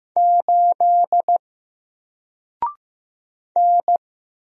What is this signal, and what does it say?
Morse code, heard as a keyed tone, about 15 wpm, 700 Hz. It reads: 8 N
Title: Single Letter-Number Alt 15wpm